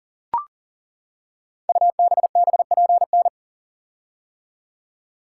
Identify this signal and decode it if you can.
Morse code, heard as a keyed tone, about 41 wpm, 700 Hz. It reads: UBBPN
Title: Code Groups - 5 Letter:Numbers - 40wpm